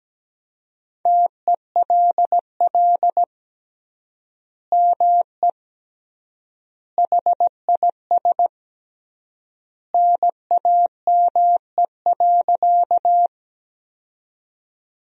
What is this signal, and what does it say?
Morse code, heard as a keyed tone, about 17 wpm, 700 Hz. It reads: TELL ME HIS NAME.
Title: 3x Word-Spacing - Sentences from Top 200 Words - 17wpm